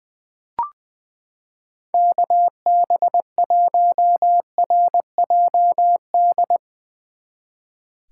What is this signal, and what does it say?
Morse code, heard as a keyed tone, about 20 wpm, 700 Hz. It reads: KB1RJD
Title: 2023 SOTA Chaser Call Signs - 20wpm